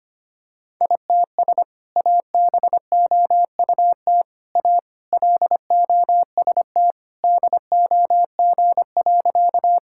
Morse code, heard as a keyed tone, about 25 wpm, 700 Hz. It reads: ITS ABOUT A LOST DOG.
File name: practice / 25wpm / Sentences with 2-Word Phrases - 25wpm